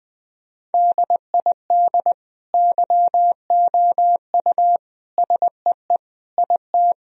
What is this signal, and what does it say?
Morse code, heard as a keyed tone, about 20 wpm, 700 Hz. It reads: DID YOU SEE IT
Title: Sentences from Top 1000 Words - Review Part 1 - 20wpm